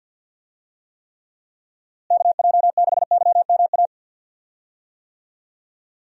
Morse code, heard as a keyed tone, about 46 wpm, 700 Hz. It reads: KJ6YMA
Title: USA Call Signs - 6 Characters - Repeated 3x - 50wpm